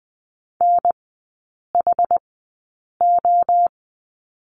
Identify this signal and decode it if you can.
Morse code, heard as a keyed tone, about 20 wpm, spaced as for 8 wpm, 700 Hz. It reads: NHO
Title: 093 Lesson 16e - Letter c - Three Characters 20wpm